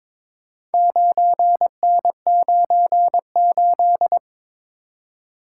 Morse code, heard as a keyed tone, about 22 wpm, 700 Hz. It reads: 9N98
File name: practice / 22wpm / Four Letter-Number Combinations - 22wpm